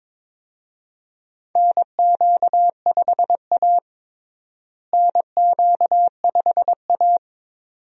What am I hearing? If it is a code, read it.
Morse code, heard as a keyed tone, about 22 wpm, 700 Hz. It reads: NQ5A NQ5A
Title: USA Call Signs - 4 Characters - Repeated 2x - 22wpm